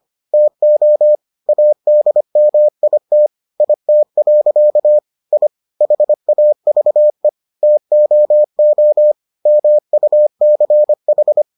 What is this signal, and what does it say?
Morse code, heard as a keyed tone, about 25 wpm, 600 Hz. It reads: TO ADMIT IT. I HAVE TOO MUCH